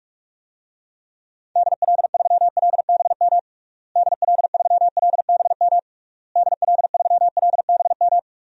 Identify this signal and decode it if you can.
Morse code, heard as a keyed tone, about 42 wpm, 700 Hz. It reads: DL3LBM DL3LBM DL3LBM
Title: DXCC-100 Call Signs - 6 Characters - Repeated 3x - 45wpm